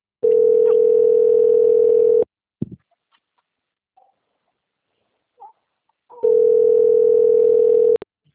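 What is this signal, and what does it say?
Ringback tone (North America), 440 + 480 Hz.
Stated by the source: Call progress tone on an outbound telephone call, Piracicaba.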